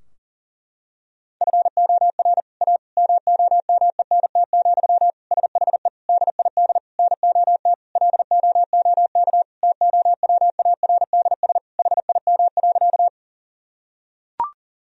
Morse code, heard as a keyed tone, about 37 wpm, 700 Hz. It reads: FOR A MOMENT, SHE DID NOT LOOK TOWARDS HIM.
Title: Sentences from Top 1000 Words - Review Part 2 - 40wpm